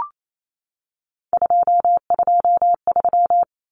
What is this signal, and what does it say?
Morse code, heard as a keyed tone, about 28 wpm, 700 Hz. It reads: 223
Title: Common Cut Numbers - 3-Digit 28wpm